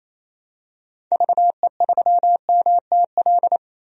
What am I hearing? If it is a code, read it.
Morse code, heard as a keyed tone, about 28 wpm, 700 Hz. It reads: VE3MTL
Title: DXCC-100 Call Signs - 6 Characters - Repeated 2x - 28wpm